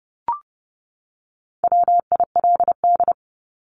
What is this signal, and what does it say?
Morse code, heard as a keyed tone, about 30 wpm, 700 Hz. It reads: WILD
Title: Top 1501-1750 Words 30wpm